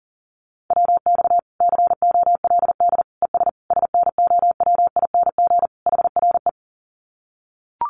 Morse code, heard as a keyed tone, about 37 wpm, 700 Hz. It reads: WX COLD ES SNOWING HRE
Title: Common QSO Phrases 40wpm